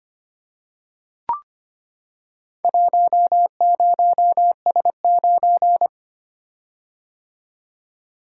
Morse code, heard as a keyed tone, about 25 wpm, 690 Hz. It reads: 10S9
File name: practice / 25wpm / Four Letter-Number Combinations - 25wpm